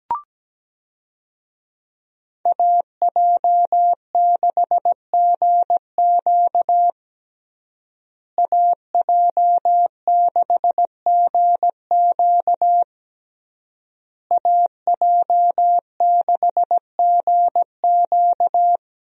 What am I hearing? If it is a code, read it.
Morse code, heard as a keyed tone, about 17 wpm, 700 Hz. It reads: AJ6GQ AJ6GQ AJ6GQ
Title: USA Call Signs - 5 Characters - Repeated 3x - 17wpm